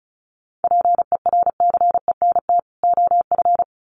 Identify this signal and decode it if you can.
Morse code, heard as a keyed tone, about 34 wpm, 700 Hz. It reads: PERCENT OF
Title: Sets of 2 Words 35wpm